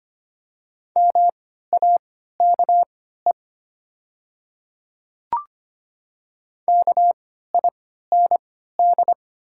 Morse code, heard as a keyed tone, about 25 wpm, spaced as for 14 wpm, 690 Hz. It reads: MAKE KIND
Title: 3x Character Spacing - Top 200 Words - 25wpm